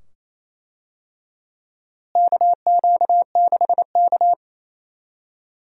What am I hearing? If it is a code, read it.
Morse code, heard as a keyed tone, about 28 wpm, 700 Hz. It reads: KQ6K
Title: USA Call Signs - 4 Characters - Repeated 2x - 28wpm